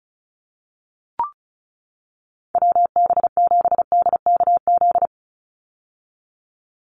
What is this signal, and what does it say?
Morse code, heard as a keyed tone, about 34 wpm, 700 Hz. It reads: WB7DKZ
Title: Call Signs - 35wpm